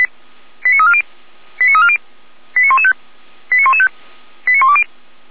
Five-tone selective call (ZVEI), heard as quick runs of five tones, tones of about 80 ms. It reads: 89340 89340 89115 89115 89120